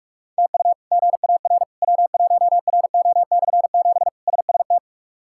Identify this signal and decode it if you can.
Morse code, heard as a keyed tone, about 46 wpm, 700 Hz. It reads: TU GAR W1RO/7 SST
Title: K1USN SST QSO Practice - 45wpm